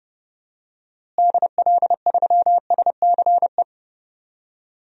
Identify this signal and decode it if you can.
Morse code, heard as a keyed tone, about 32 wpm, 700 Hz. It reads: DL3SCE
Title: DXCC-100 Call Signs - 6 Characters - Repeated 3x - 30wpm